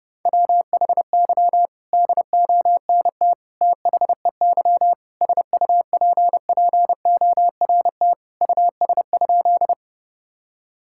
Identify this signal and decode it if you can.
Morse code, heard as a keyed tone, about 30 wpm, 700 Hz. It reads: WHY DONT THEY SUPPORT US?
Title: Sentences from Top 1000 Words - Review Part 5 - 30wpm